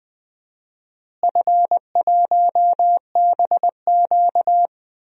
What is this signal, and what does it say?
Morse code, heard as a keyed tone, about 20 wpm, 700 Hz. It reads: F1BQ
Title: DXCC-100 Call Signs - 4 Characters - Repeated 2x - 20wpm